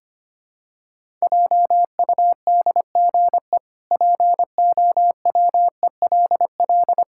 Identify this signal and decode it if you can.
Morse code, heard as a keyed tone, about 25 wpm, 700 Hz. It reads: JUDGE POWELL
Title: Pangram Challenge v2 - 25wpm